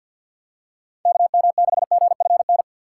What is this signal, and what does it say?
Morse code, heard as a keyed tone, about 45 wpm, 700 Hz. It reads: KM6GFN
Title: USA Call Signs - 6 Characters - 3x Character Spacing - 50wpm